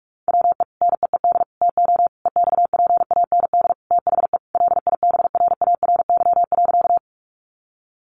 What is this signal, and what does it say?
Morse code, heard as a keyed tone, about 44 wpm, 700 Hz. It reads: WE NEED TO EXPAND THE LIBRARY.
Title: Sentences from Top 1400 Words 45wpm